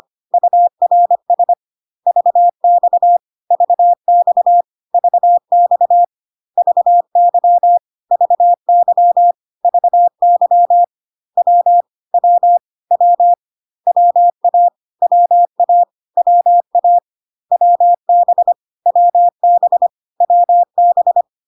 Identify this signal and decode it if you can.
Morse code, heard as a keyed tone, about 25 wpm, 700 Hz. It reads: URS VX VX VX VY VY VY W W W WA WA WA WB WB WB
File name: CW abbreviations alphabetical repeated 3 times 25 25wpm